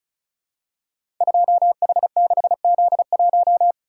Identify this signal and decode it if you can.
Morse code, heard as a keyed tone, about 36 wpm, 700 Hz. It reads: 2H6Z1
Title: Code Groups - 5 Letter:Numbers - 35wpm